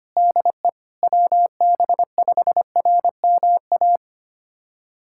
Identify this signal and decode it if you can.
Morse code, heard as a keyed tone, about 25 wpm, 700 Hz. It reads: DE WB5RMA
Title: USA Callsign DE Callsign - 6 Characters - 25wpm